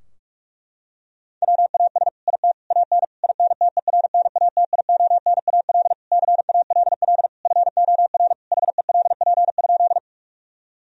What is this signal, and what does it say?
Morse code, heard as a keyed tone, about 46 wpm, 700 Hz. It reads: WAS IT AN INTERNATIONAL CALL FOR HELP?
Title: Sentences from Top 1000 Words - Review Part 4 - 45wpm